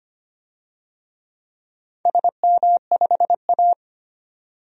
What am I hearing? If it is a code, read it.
Morse code, heard as a keyed tone, about 25 wpm, 700 Hz. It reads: SM5A
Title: DXCC-100 Call Signs - 4 Characters - 25wpm